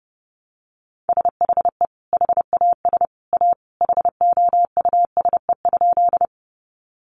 Morse code, heard as a keyed tone, about 30 wpm, 700 Hz. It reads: SHE HAS A HOUSE?
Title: Sentences from Top 200 Words 30wpm